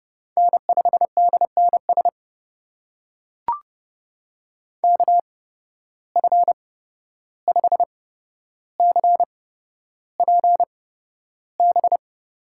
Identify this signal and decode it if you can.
Morse code, heard as a keyed tone, about 30 wpm, spaced as for 12 wpm, 700 Hz. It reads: N5DNS KF5CPB
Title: USA Call Signs - 8x Character Spacing - 30wpm